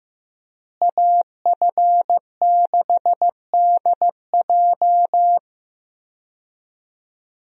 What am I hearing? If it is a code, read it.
Morse code, heard as a keyed tone, about 15 wpm, 700 Hz. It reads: AF6DJ
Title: USA Call Signs - 5 Characters - Repeated 3x - 15wpm